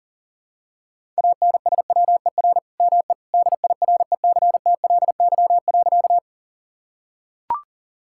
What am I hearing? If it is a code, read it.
Morse code, heard as a keyed tone, about 40 wpm, 700 Hz. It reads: ANSWER ME DIRECTLY.
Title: Sentences from Top 1000 Words - Review Part 5 - 40wpm